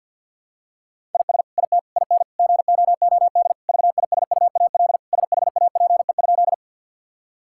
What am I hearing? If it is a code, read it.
Morse code, heard as a keyed tone, about 53 wpm, 700 Hz. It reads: IS IT IN GOOD VISUAL SHAPE?
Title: Sentences from Top 1000 Words - Review Part 6 - 50wpm